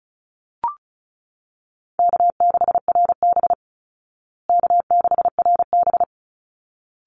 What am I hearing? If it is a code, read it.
Morse code, heard as a keyed tone, about 34 wpm, 700 Hz. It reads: K6RB K6RB
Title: Easy Call Signs Repeated 2x - 35wpm